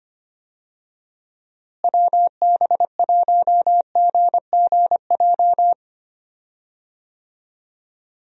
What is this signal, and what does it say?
Morse code, heard as a keyed tone, about 25 wpm, 700 Hz. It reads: WB1GGJ
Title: USA Call Signs - 6 Characters - Repeated 2x - 25wpm